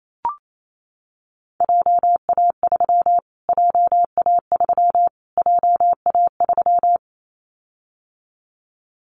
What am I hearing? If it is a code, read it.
Morse code, heard as a keyed tone, about 28 wpm, 700 Hz. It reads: JA3 JA3 JA3
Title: Callsign Prefixes for Top 25 DXCC - Repeated 3x - 28wpm